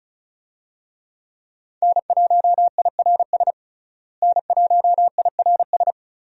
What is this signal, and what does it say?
Morse code, heard as a keyed tone, about 36 wpm, 700 Hz. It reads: N1IRS N1IRS
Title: USA Call Signs - 5 Characters - Repeated 3x - 35wpm